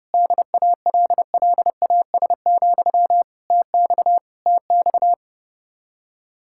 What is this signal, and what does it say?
Morse code, heard as a keyed tone, about 33 wpm, 700 Hz. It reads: DALLAS, TX TX